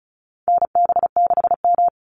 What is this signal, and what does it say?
Morse code, heard as a keyed tone, about 34 wpm, 700 Hz. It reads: NB6M
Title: Call Signs Repeated 3x - 35wpm